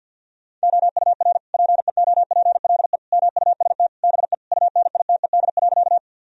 Morse code, heard as a keyed tone, about 50 wpm, 700 Hz. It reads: OUR PEOPLE MUST BE UNITED.
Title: Sentences from Top 1000 Words - Review Part 4 - 50wpm